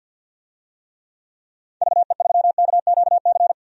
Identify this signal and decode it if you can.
Morse code, heard as a keyed tone, about 50 wpm, 700 Hz. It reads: VE3KQC
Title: DXCC-100 Call Signs - 6 Characters - 50wpm